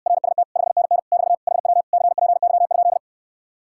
Morse code, heard as a keyed tone, about 69 wpm, 700 Hz. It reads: RST 5NN <BT> HW CPY?